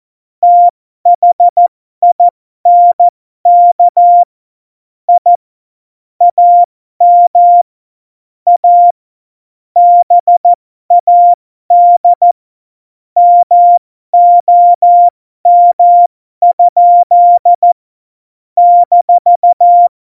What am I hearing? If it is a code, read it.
Morse code, heard as a keyed tone, about 14 wpm, spaced as for 12 wpm, 700 Hz. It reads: THINK I AM A BAD MOM? -
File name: Morsekurs / Witze